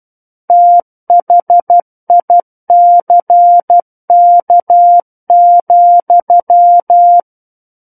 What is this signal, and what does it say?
Morse code, heard as a keyed tone, about 12 wpm, 700 Hz. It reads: THICK,
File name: Book One / 03-Chapter 3 - 12 WPM